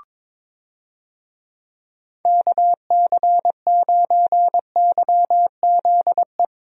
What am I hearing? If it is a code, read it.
Morse code, heard as a keyed tone, about 22 wpm, 700 Hz. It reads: KC9YZE
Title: USA Call Signs - 6 Characters - Repeated 3x - 22wpm